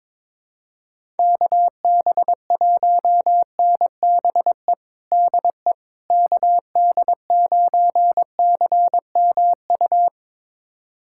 Morse code, heard as a keyed tone, about 22 wpm, 700 Hz. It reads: KB1NBE DE KD9CMU
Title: USA Callsign DE Callsign - 6 Characters - 22wpm